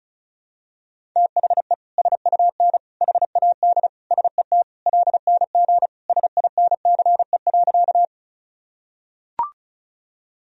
Morse code, heard as a keyed tone, about 34 wpm, 700 Hz. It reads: THE SUN HAD SET LNG SINCE.
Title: Sentences from Top 300 Words and Abbreviations 35wpm